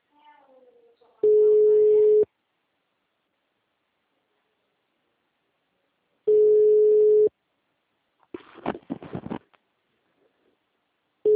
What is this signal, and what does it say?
Ringback tone (Europe, 425 Hz), with speech.